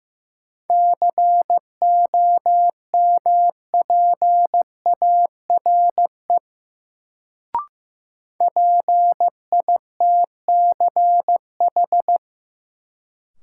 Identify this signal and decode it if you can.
Morse code, heard as a keyed tone, about 15 wpm, 700 Hz. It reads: COMPARE PITCH
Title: Sentences from Top 1300 Words - New Word Drill - 15wpm